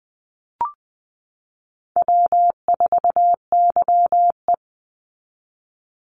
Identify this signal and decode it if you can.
Morse code, heard as a keyed tone, about 20 wpm, 700 Hz. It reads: W4YE
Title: Easy Call Signs - 20wpm